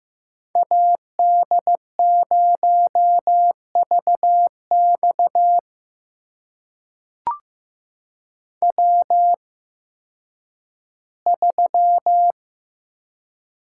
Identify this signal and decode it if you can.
Morse code, heard as a keyed tone, about 15 wpm, spaced as for 10 wpm, 700 Hz. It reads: AD0VX W3
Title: USA Call Signs - 5 Characters - 8x Character Spacing - 15wpm